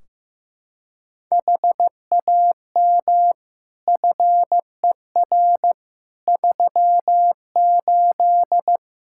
Morse code, heard as a keyed tone, about 15 wpm, 700 Hz. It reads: HAM FER 38
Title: K1USN SST QSO Practice 15wpm